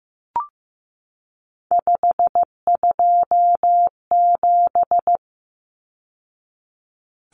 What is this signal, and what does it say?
Morse code, heard as a keyed tone, about 15 wpm, 700 Hz. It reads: 527
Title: Numbers - 3-Digit 15wpm